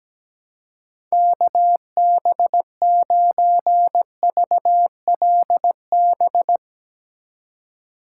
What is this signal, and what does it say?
Morse code, heard as a keyed tone, about 17 wpm, 700 Hz. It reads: KB9VLB
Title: USA Call Signs - 6 Characters - 8x Character Spacing - 17wpm